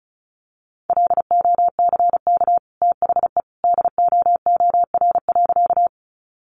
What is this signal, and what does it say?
Morse code, heard as a keyed tone, about 34 wpm, 700 Hz. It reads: LOCK THE DOOR.
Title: Sentences from Top 1300 Words 35wpm